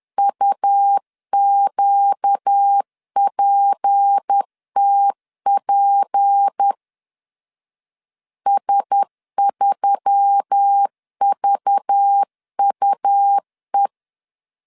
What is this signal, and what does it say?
Morse code, heard as a keyed tone, about 10 wpm, 800 Hz. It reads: UQPTP S3VUE
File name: LEZIONI / 6 PKX0 / 6 P 10wpm 2432